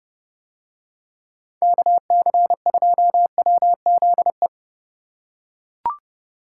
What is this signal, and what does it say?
Morse code, heard as a keyed tone, about 30 wpm, 700 Hz. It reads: KC2WZE